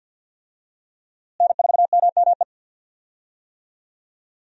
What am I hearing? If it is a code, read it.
Morse code, heard as a keyed tone, about 46 wpm, 700 Hz. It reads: N4MME
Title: USA Call Signs - 5 Characters - Repeated 3x - 50wpm